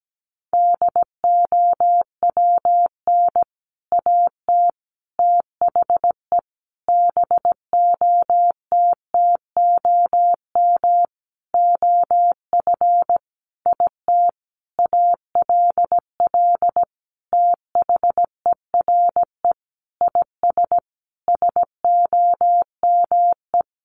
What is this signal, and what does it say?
Morse code, heard as a keyed tone, about 17 wpm, 700 Hz. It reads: DOWN AT THE BOTTOM OF IT ALL THERE IS SOME
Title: Sentences from Top 1100 Words 17wpm